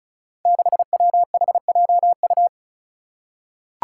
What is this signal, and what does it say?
Morse code, heard as a keyed tone, about 36 wpm, 700 Hz. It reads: 6WHJU
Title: Five Challenging Letter-Number Combinations 35wpm